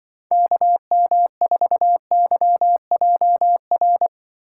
Morse code, heard as a keyed tone, about 24 wpm, 700 Hz. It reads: KM4YJR
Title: Speed Racing - USA 2x3 Call Signs 20wpm